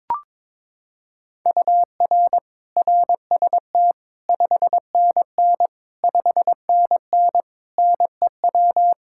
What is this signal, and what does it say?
Morse code, heard as a keyed tone, about 22 wpm, 700 Hz. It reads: UR RST 5NN 5NN NEW
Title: Practice QSO - Exchange 1 - Beginning - 22wpm